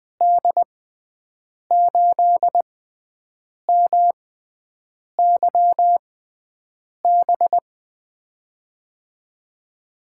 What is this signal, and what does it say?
Morse code, heard as a keyed tone, about 20 wpm, spaced as for 7 wpm, 700 Hz. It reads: D8MYB
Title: USA Call Signs - 6 Characters - 6x Character Spacing - 20wpm